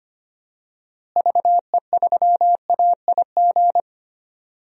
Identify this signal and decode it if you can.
Morse code, heard as a keyed tone, about 25 wpm, 700 Hz. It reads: VE3AIG
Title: DXCC-100 Call Signs - 6 Characters - Repeated 3x - 25wpm